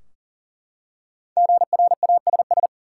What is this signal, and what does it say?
Morse code, heard as a keyed tone, about 37 wpm, 700 Hz. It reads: GRASS
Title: Five Letter Words - 40wpm